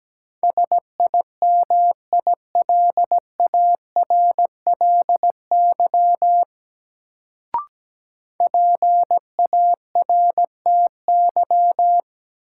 Morse code, heard as a keyed tone, about 17 wpm, 700 Hz. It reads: SIMILARLY PARTY
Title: Sentences from Top 1400 Words - New Word Drill - 17wpm